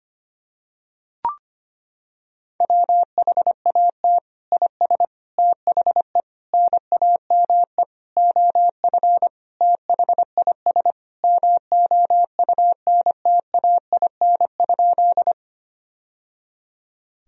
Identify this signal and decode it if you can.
Morse code, heard as a keyed tone, about 25 wpm, 700 Hz. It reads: WHAT IS THE NAME OF THIS MOUNTAIN?
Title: Sentences from Top 1000 Words - Review Part 1 - 25wpm